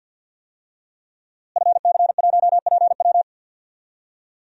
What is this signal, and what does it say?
Morse code, heard as a keyed tone, about 52 wpm, 700 Hz. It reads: FC1PW
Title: DXCC-100 Call Signs - 5 Characters - Repeated 3x - 50wpm